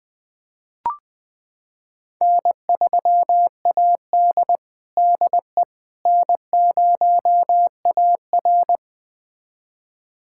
Morse code, heard as a keyed tone, about 21 wpm, 700 Hz. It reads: N3AD DE N0AR
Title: USA Callsign DE Callsign - 4 Characters - 20wpm